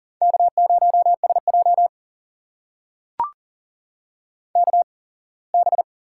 Morse code, heard as a keyed tone, about 40 wpm, spaced as for 24 wpm, 700 Hz. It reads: K0SJ KB
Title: USA Call Signs - 8x Character Spacing - 40wpm